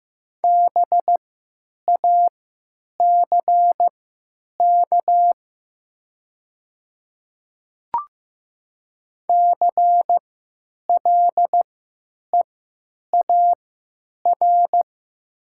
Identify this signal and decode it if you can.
Morse code, heard as a keyed tone, about 15 wpm, spaced as for 9 wpm, 700 Hz. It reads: BACK CLEAR
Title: 3x Character Spacing - Top 500 Words - 15wpm